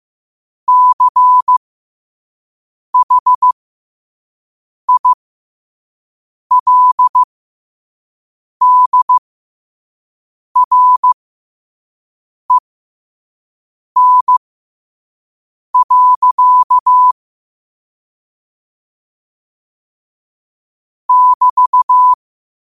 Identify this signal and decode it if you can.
Morse code, heard as a keyed tone, about 15 wpm, spaced as for 5 wpm, 1000 Hz. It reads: CHILDREN. <BT>
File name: qotdfiles / 5wpm